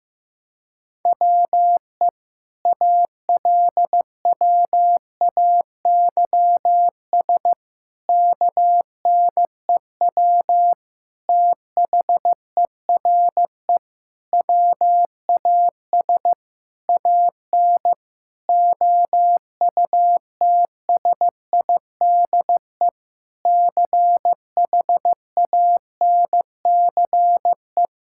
Morse code, heard as a keyed tone, about 15 wpm, 700 Hz. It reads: WE ALWAYS KNEW THERE WAS AN OUTSIDE CHANCE
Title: Sentences from Top 1000 Words - Review Part 6 - 15wpm